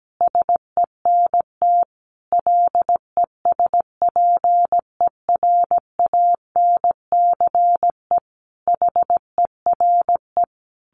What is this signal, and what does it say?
Morse code, heard as a keyed tone, about 17 wpm, 700 Hz. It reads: SENT LESPERANCE HERE
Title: A Sound Of Thunder 17wpm